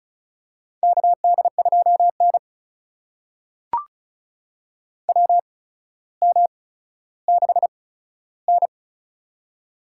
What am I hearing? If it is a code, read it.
Morse code, heard as a keyed tone, about 36 wpm, spaced as for 15 wpm, 700 Hz. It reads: KD2N WM6N
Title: USA Call Signs - 4 Characters - 8x Character Spacing - 35wpm